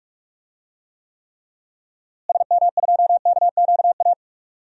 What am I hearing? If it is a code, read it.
Morse code, heard as a keyed tone, about 45 wpm, 700 Hz. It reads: SM2KQA
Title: DXCC-100 Call Signs - 6 Characters - Repeated 3x - 45wpm